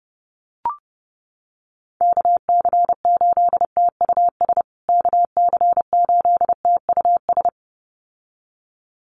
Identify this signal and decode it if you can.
Morse code, heard as a keyed tone, about 30 wpm, 700 Hz. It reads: KC8TUS KC8TUS
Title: USA Call Signs Repeated 2x 30wpm